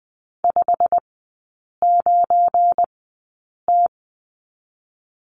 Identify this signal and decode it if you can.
Morse code, heard as a keyed tone, about 20 wpm, spaced as for 8 wpm, 700 Hz. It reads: 59T
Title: 184 Lesson 30e - Letter b - Three Characters 20wpm